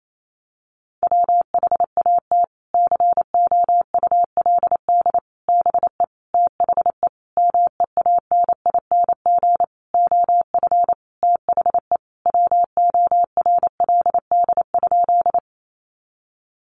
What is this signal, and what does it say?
Morse code, heard as a keyed tone, about 28 wpm, 700 Hz. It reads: WHAT COULD BE THE MEANING OF THE WORLD?
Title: Sentences from Top 900 Words 28wpm